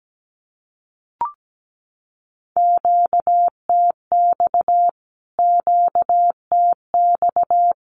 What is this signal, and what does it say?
Morse code, heard as a keyed tone, about 17 wpm, 700 Hz. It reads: QTX QTX
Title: Q-Codes Repeated 3x 17wpm